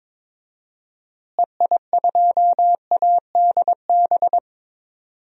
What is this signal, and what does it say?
Morse code, heard as a keyed tone, about 22 wpm, 700 Hz. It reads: EI2ADB
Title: DXCC-100 Call Signs - 6 Characters - Repeated 3x - 22wpm